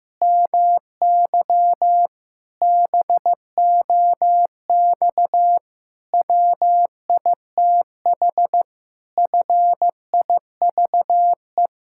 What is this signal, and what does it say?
Morse code, heard as a keyed tone, about 16 wpm, 700 Hz. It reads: MY BOX WITH FIVE
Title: Pangram Challenge v2 - 15wpm